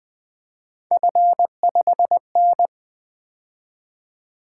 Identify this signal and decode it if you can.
Morse code, heard as a keyed tone, about 20 wpm, 700 Hz. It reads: F5N
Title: Three Letter-Number Combinations - 20wpm